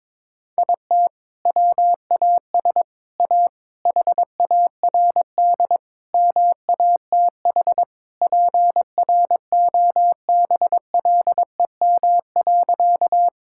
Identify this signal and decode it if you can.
Morse code, heard as a keyed tone, about 22 wpm, 700 Hz. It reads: IT WAS A HARD MATH PROBLEM.
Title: Sentences from Top 1000 Words - Review Part 6 - 22wpm